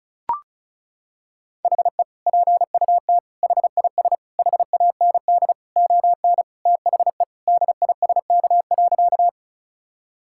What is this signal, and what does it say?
Morse code, heard as a keyed tone, about 36 wpm, 700 Hz. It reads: HE PUT HIS HAND ON THE DISK.
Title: Sentences from Top 1000 Words - Review Part 6 - 35wpm